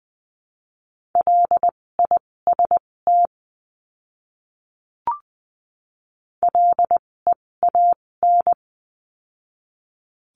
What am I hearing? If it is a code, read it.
Morse code, heard as a keyed tone, about 20 wpm, spaced as for 16 wpm, 700 Hz. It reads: LIST LEAN
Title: 256 Lesson 42a - Words 12-20wpm